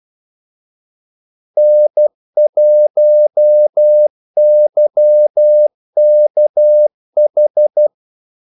Morse code, heard as a keyed tone, about 12 wpm, 600 Hz. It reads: N1YKH
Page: 2 ekstra ordmellemrum mellem kaldesignalerne